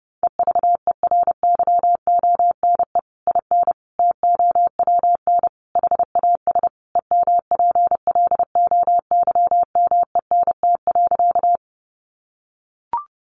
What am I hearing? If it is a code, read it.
Morse code, heard as a keyed tone, about 30 wpm, 700 Hz. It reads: EVERYONE IN TOWN HAS EMPLOYMENT.
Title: Sentences from Top 1500 Words 30wpm